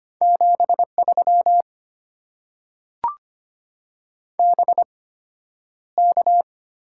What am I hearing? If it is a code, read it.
Morse code, heard as a keyed tone, about 25 wpm, spaced as for 11 wpm, 700 Hz. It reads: 73 BK
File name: Top 90 QSO Elements - 8x Character Spacing - 25wpm